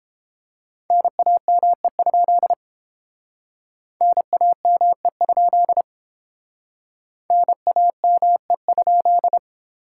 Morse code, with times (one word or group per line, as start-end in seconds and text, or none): 0.9-2.5 NAME?
4.0-5.8 NAME?
7.3-9.4 NAME?